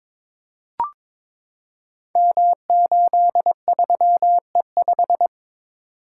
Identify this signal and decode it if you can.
Morse code, heard as a keyed tone, about 22 wpm, 700 Hz. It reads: M83E5
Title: Code Groups - 5 Letter:Numbers - 22wpm